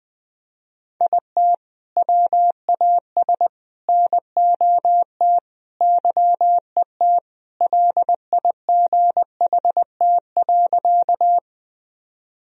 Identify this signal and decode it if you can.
Morse code, heard as a keyed tone, about 20 wpm, 700 Hz. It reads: IT WAS NOT YET LIGHT.
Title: Sentences from Top 1000 Words - Review Part 1 - 20wpm